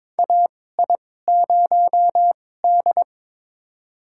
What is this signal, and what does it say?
Morse code, heard as a keyed tone, about 22 wpm, spaced as for 16 wpm, 700 Hz. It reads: AI0D
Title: USA Call Signs - 4 Characters - 2x Character Spacing - 22wpm